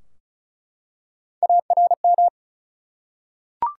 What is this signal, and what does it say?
Morse code, heard as a keyed tone, about 36 wpm, 700 Hz. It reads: ARM
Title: Three Letter Words - 35wpm